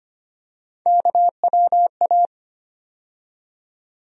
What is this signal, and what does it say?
Morse code, heard as a keyed tone, about 25 wpm, 700 Hz. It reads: KWA
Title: USA Call Sign Postfixes - 3 Letters 25wpm